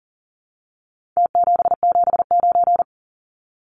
Morse code, heard as a keyed tone, about 40 wpm, 700 Hz. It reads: T779
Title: Numbers - Contest Style - 4-Digit 40wpm